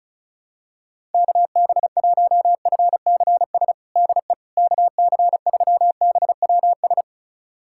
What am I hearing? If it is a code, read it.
Morse code, heard as a keyed tone, about 36 wpm, 700 Hz. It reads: KB1FCS DE KC3BWS
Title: USA Callsign DE Callsign 35wpm